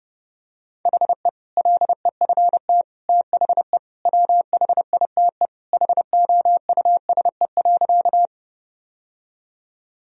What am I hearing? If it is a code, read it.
Morse code, heard as a keyed tone, about 30 wpm, 700 Hz. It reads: HE LEFT THE WHITE HOUSE.
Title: Sentences from Top 1000 Words - Review Part 1 - 30wpm